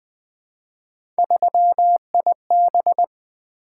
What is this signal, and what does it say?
Morse code, heard as a keyed tone, about 20 wpm, 700 Hz. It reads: 3IB
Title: Three Letter-Number Combinations - 20wpm